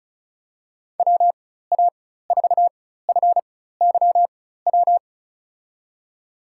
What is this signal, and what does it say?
Morse code, heard as a keyed tone, about 34 wpm, spaced as for 16 wpm, 700 Hz. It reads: WA4FYW
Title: USA Call Signs - 6 Characters - 4x Character Spacing - 35wpm